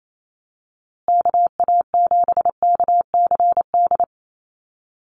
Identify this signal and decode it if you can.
Morse code, heard as a keyed tone, about 28 wpm, 700 Hz. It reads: KA7KCD
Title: USA Call Signs Repeated 3x 28wpm